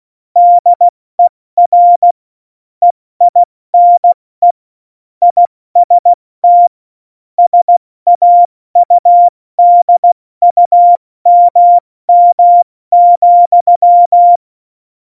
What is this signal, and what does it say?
Morse code, heard as a keyed tone, about 16 wpm, 700 Hz. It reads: DER EINE IST SAUDUMM,
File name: Morsekurs / Witze / Witz 08. August 17, Tempo 70 F80